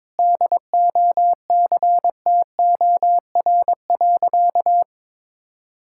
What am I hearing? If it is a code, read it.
Morse code, heard as a keyed tone, about 21 wpm, 700 Hz. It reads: DOCTOR.
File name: practice / 22wpm / Sentences with 3-Word Phrases - 22wpm